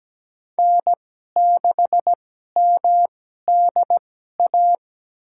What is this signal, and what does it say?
Morse code, heard as a keyed tone, about 17 wpm, spaced as for 12 wpm, 700 Hz. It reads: N6MDA
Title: USA Call Signs - 5 Characters - 2x Character Spacing - 17wpm